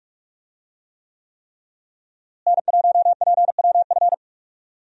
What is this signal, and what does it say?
Morse code, heard as a keyed tone, about 44 wpm, 700 Hz. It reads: N1PWF
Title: USA Call Signs - 5 Characters - Repeated 3x - 45wpm